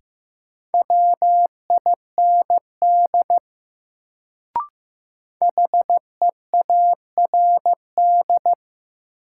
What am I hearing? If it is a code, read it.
Morse code, heard as a keyed tone, about 15 wpm, 700 Hz. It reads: WIND HEARD
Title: Sentences from Top 400 Words - New Word Drill - 15wpm